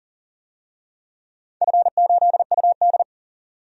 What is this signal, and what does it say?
Morse code, heard as a keyed tone, about 41 wpm, 700 Hz. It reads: F8UD
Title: DXCC-100 Call Signs - 4 Characters - 40wpm